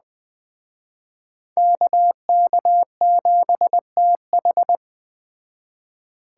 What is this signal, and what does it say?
Morse code, heard as a keyed tone, about 20 wpm, 700 Hz. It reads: KK7TH
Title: Speed Racing - USA 2x2 Call Signs 17wpm